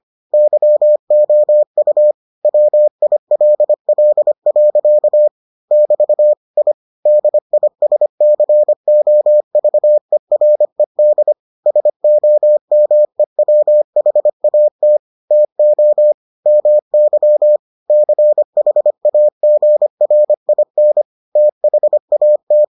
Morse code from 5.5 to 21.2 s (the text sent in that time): <BT> I DISCOVERED SOMEWHAT TO MY CHAGRIN